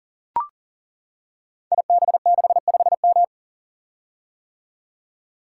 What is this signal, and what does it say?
Morse code, heard as a keyed tone, about 39 wpm, 690 Hz. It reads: IB65M
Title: Code Groups - 5 Letter:Numbers - 40wpm